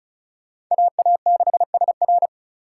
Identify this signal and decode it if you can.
Morse code, heard as a keyed tone, about 44 wpm, 700 Hz. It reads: AA6SR